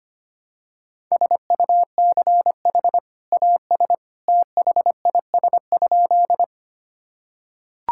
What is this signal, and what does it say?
Morse code, heard as a keyed tone, about 25 wpm, 700 Hz. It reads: SUCH AS THIS?
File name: practice / 25wpm / Sentences with 2-Word Phrases - 25wpm